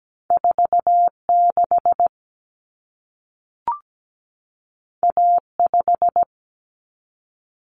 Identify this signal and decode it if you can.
Morse code, heard as a keyed tone, about 17 wpm, 700 Hz. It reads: 46 A5
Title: Common Cut Numbers - 2-Digit 17wpm